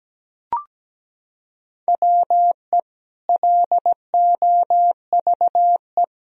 Morse code, heard as a keyed tone, about 17 wpm, 700 Hz. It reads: WE LOVE
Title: Sentences from Top 1000 Words - Review Part 1 - 17wpm